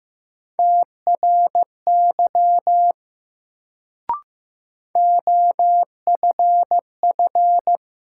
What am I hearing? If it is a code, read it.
Morse code, heard as a keyed tone, about 15 wpm, 700 Hz. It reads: TRY OFF
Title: Sentences from Top 200 Words - New Word Drill - 15wpm